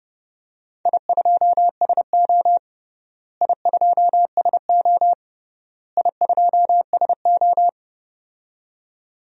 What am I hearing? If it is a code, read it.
Morse code, heard as a keyed tone, about 32 wpm, 700 Hz. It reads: I2SO I2SO I2SO
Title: DXCC-100 Call Signs - 4 Characters - Repeated 3x - 30wpm